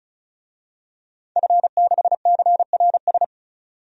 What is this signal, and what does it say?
Morse code, heard as a keyed tone, about 34 wpm, 700 Hz. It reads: F6CRS
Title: DXCC-100 Call Signs - 5 Characters - 35wpm